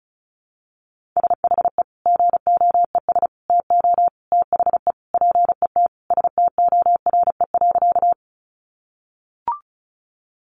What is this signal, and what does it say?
Morse code, heard as a keyed tone, about 34 wpm, 700 Hz. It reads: SHE GOES TO THE PET STORE.
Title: Easy Phrases 35wpm